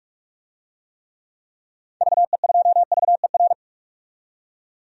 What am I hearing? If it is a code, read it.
Morse code, heard as a keyed tone, about 46 wpm, 700 Hz. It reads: VE2VER
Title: DXCC-100 Call Signs - 6 Characters - 45wpm